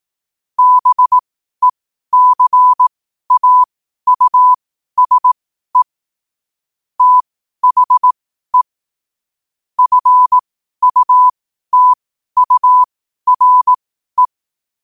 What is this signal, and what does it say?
Morse code, heard as a keyed tone, about 18 wpm, spaced as for 12 wpm, 1000 Hz. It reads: BECAUSE THE FUTURE
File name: qotdfiles / 12wpm